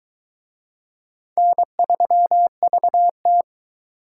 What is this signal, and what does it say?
Morse code, heard as a keyed tone, about 22 wpm, 700 Hz. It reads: N3VT